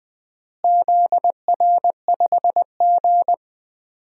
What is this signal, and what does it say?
Morse code, heard as a keyed tone, about 20 wpm, 700 Hz. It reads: ZR5G
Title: DXCC-100 Call Signs - 4 Characters - Repeated 3x - 20wpm